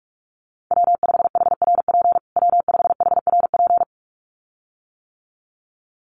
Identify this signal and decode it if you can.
Morse code, heard as a keyed tone, about 44 wpm, 700 Hz. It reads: W5HRP W5HRP
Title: USA Call Signs Repeated 2x 45wpm